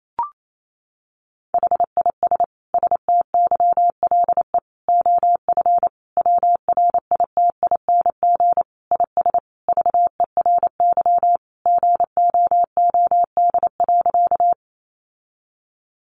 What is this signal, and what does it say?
Morse code, heard as a keyed tone, about 28 wpm, 700 Hz. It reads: HIS STYLE OF WRITING IS VERY GOOD.
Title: Sentences from Top 800 Words 28wpm